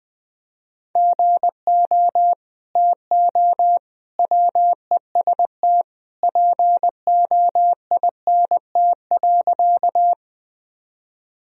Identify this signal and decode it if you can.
Morse code, heard as a keyed tone, about 20 wpm, 700 Hz. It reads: GO TO WEST POINT.
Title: Sentences from Top 1000 Words - Review Part 2 - 20wpm